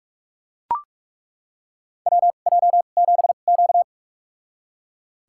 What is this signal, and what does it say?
Morse code, heard as a keyed tone, about 42 wpm, spaced as for 31 wpm, 700 Hz. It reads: WJ7Q
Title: USA Call Signs - 2x Character Spacing - 45wpm